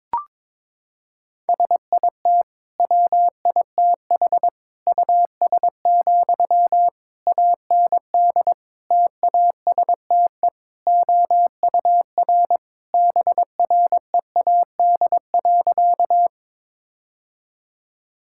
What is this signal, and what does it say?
Morse code, heard as a keyed tone, about 22 wpm, 700 Hz. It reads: SIT WITH US, AND TASTE OUR BREAD.
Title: Sentences from Top 1600 Words 22wpm